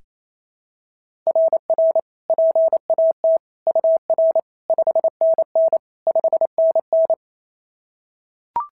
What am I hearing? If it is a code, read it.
Morse code, heard as a keyed tone, about 28 wpm, 690 Hz. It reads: RR PAT UR 5NN 5NN
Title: SOTA QSO Practice 28wpm